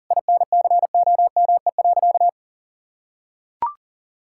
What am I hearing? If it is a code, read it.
Morse code, heard as a keyed tone, about 40 wpm, 700 Hz. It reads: INCOME.
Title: Sentences from Top 1000 Words - Review Part 5 - 40wpm